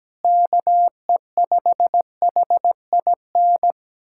Morse code, heard as a keyed tone, about 17 wpm, 700 Hz. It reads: KE5HIN
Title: USA Callsign DE Callsign 17wpm